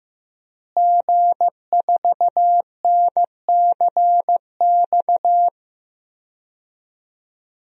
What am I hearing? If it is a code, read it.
Morse code, heard as a keyed tone, about 15 wpm, 700 Hz. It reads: G4NCX
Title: DXCC-100 Call Signs - 5 Characters - Repeated 2x - 15wpm